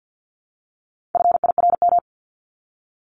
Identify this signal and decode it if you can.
Morse code, heard as a keyed tone, about 50 wpm, 700 Hz. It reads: FIRM
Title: Top 1251-1500 Words 50wpm